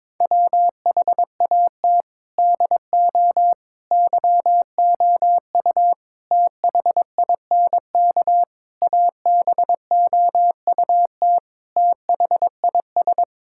Morse code, heard as a keyed tone, about 22 wpm, 700 Hz. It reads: WHAT DO YOU THINK ABOUT THIS
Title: Sentences from Top 1000 Words - Review Part 5 - 22wpm